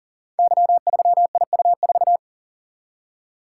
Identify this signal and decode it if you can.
Morse code, heard as a keyed tone, about 42 wpm, 700 Hz. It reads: Y3IU4
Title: Code Groups - 5 Letter:Numbers - 40wpm